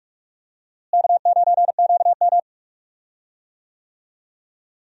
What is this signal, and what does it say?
Morse code, heard as a keyed tone, about 46 wpm, 700 Hz. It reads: K9QM
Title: USA Call Signs - 4 Characters - Repeated 2x - 45wpm